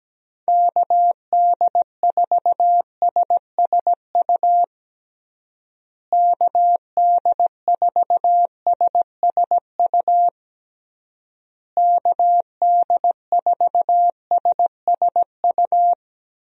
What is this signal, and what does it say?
Morse code, heard as a keyed tone, about 17 wpm, 700 Hz. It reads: KD4SSU KD4SSU KD4SSU
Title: USA Call Signs - 6 Characters - Repeated 3x - 17wpm